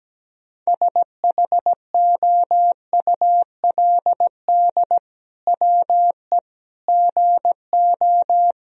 Morse code, heard as a keyed tone, about 17 wpm, 700 Hz. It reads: SHOULD WE GO
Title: Sentences from Top 1000 Words - Review Part 3 - 17wpm